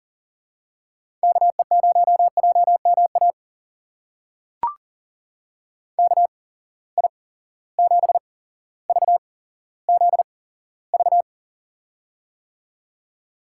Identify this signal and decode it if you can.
Morse code, heard as a keyed tone, about 39 wpm, spaced as for 17 wpm, 700 Hz. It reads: KE0JMA KI7VZV
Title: USA Call Signs - 6 Characters - 8x Character Spacing - 40wpm